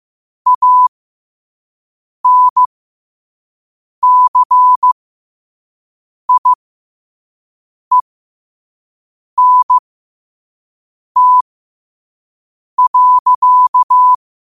Morse code, heard as a keyed tone, about 15 wpm, spaced as for 5 wpm, 1000 Hz. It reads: ANCIENT.
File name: qotdfiles / 5wpm